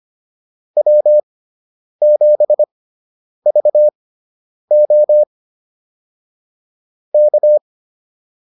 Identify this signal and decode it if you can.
Morse code, heard as a keyed tone, about 25 wpm, spaced as for 9 wpm, 600 Hz. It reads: W7VO K